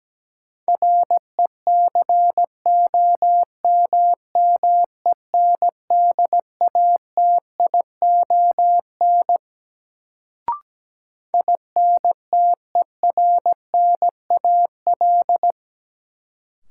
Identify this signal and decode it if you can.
Morse code, heard as a keyed tone, about 17 wpm, 700 Hz. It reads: RECOMMENDATION INTERNAL
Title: Sentences from Top 1400 Words - New Word Drill - 17wpm